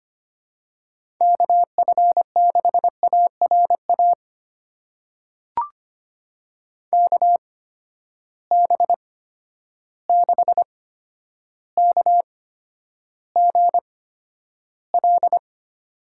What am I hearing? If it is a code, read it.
Morse code, heard as a keyed tone, about 25 wpm, spaced as for 11 wpm, 700 Hz. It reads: KF6ARA KB6KGL
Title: USA Call Signs - 6 Characters - 8x Character Spacing - 25wpm